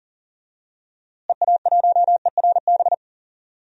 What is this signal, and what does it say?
Morse code, heard as a keyed tone, about 40 wpm, 700 Hz. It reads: EA1ERB
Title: DXCC-100 Call Signs - 6 Characters - Repeated 3x - 40wpm